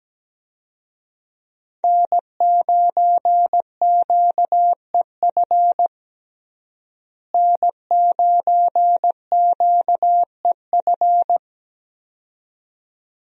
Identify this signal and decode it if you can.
Morse code, heard as a keyed tone, about 17 wpm, 700 Hz. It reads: N9QEF N9QEF
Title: USA Call Signs - 5 Characters - Repeated 2x - 17wpm